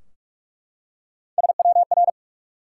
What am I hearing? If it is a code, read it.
Morse code, heard as a keyed tone, about 42 wpm, 700 Hz. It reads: SWR
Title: Top 180 QSO Elements 45wpm